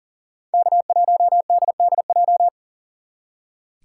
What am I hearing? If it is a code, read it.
Morse code, heard as a keyed tone, about 51 wpm, 700 Hz. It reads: K1DDJ